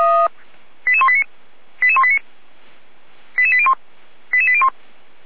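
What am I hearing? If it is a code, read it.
Five-tone selective call (ZVEI), heard as quick runs of five tones, tones of about 70 ms. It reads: 88189 88189 88801 88801
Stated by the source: ZVEI five-tone selective call 88189, received via SDR.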